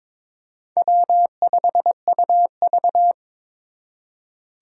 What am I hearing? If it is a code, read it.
Morse code, heard as a keyed tone, about 26 wpm, 700 Hz. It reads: W5UV
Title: Speed Racing - USA 1x2 Call Signs 22wpm